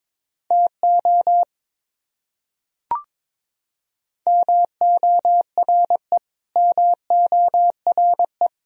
Morse code, heard as a keyed tone, about 22 wpm, 700 Hz. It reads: TO MORE MORE
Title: Top 100 Words Repeated 22wpm